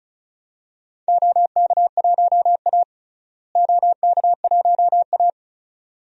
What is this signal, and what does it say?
Morse code, heard as a keyed tone, about 36 wpm, 700 Hz. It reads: OK1A OK1A
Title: DXCC-100 Call Signs - 4 Characters - Repeated 2x - 35wpm